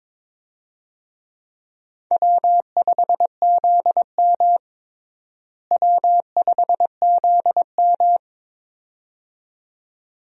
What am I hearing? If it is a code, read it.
Morse code, heard as a keyed tone, about 22 wpm, 700 Hz. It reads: W5ZM W5ZM
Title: USA Call Signs - 4 Characters - Repeated 2x - 22wpm